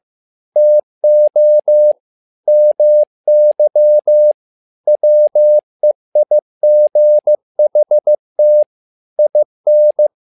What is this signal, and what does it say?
Morse code, heard as a keyed tone, about 15 wpm, 600 Hz. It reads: TO MY WEIGHT IN
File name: learning center / tower trouble / tower trouble 15wpm